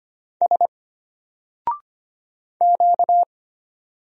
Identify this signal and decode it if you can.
Morse code, heard as a keyed tone, about 25 wpm, 700 Hz. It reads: S Q
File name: Single-Letter-Number-25wpm